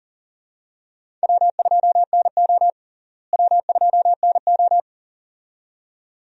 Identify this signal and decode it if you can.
Morse code, heard as a keyed tone, about 38 wpm, 700 Hz. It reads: W2NO W2NO
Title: USA Call Signs - 4 Characters - Repeated 2x - 40wpm